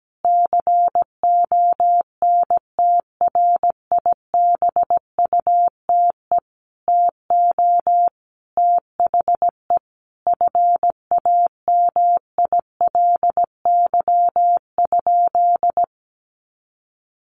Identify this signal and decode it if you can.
Morse code, heard as a keyed tone, about 17 wpm, 700 Hz. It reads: CONTRIBUTE TO THE FAMILY?
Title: Sentences from Top 1400 Words 17wpm